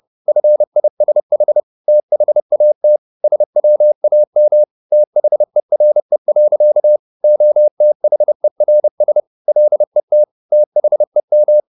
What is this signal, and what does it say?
Morse code, heard as a keyed tone, about 30 wpm, 600 Hz. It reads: FISH THAT SWAM THERE. OTHERS LET THEM
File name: the old man and the sea / the old man and the sea 1 30wpm